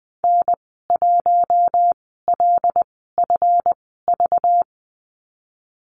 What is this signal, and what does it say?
Morse code, heard as a keyed tone, about 20 wpm, spaced as for 15 wpm, 700 Hz. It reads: N1LFV
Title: 193 Lesson 31g - Letter v - Call Signs 20wpm